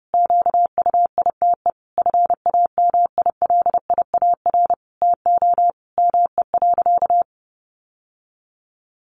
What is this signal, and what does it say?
Morse code, heard as a keyed tone, about 30 wpm, 700 Hz. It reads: QUITE FAMILIAR TO ME.
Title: Sentences from Top 1100 Words 30wpm